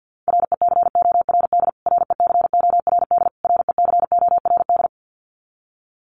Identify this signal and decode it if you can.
Morse code, heard as a keyed tone, about 49 wpm, 700 Hz. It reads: RECORD RECORD RECORD
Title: Top 500 Words Repeated 50wpm